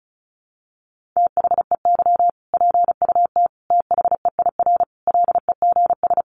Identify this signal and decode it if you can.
Morse code, heard as a keyed tone, about 34 wpm, 700 Hz. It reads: THEY PUT THEIR LEGS
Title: Easy Phrases 35wpm